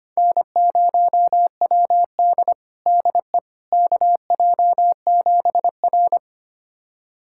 Morse code, heard as a keyed tone, about 25 wpm, 700 Hz. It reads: N0WD DE KJ7R
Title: USA Callsign DE Callsign - 4 Characters - 25wpm